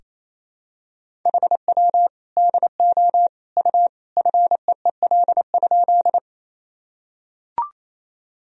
Morse code, heard as a keyed tone, about 27 wpm, 700 Hz. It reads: HW DO U FEEL?
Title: Sentences from Top 400 Words and Abbreviations 28wpm